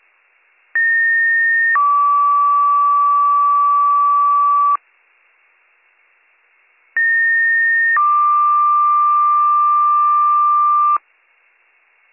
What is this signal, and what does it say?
Two-tone page, heard as a one-second tone, then a three-second tone, about 1850 then 1150 Hz, about 1850 then 1180 Hz. It is heard from 0.7 to 11.0 s.